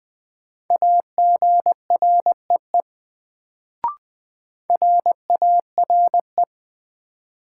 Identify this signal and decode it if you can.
Morse code, heard as a keyed tone, about 20 wpm, 700 Hz. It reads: AGREE RARE
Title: Sentences from Top 1100 Words - New Word Drill Alt - 20wpm